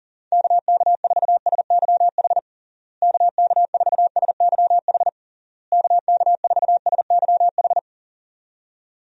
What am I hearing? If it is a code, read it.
Morse code, heard as a keyed tone, about 40 wpm, 700 Hz. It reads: KK4SYH KK4SYH KK4SYH
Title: USA Call Signs - 6 Characters - Repeated 3x - 40wpm